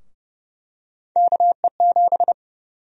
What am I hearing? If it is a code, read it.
Morse code, heard as a keyed tone, about 30 wpm, 700 Hz. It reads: KE7
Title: USA Call Sign Prefixes - 30wpm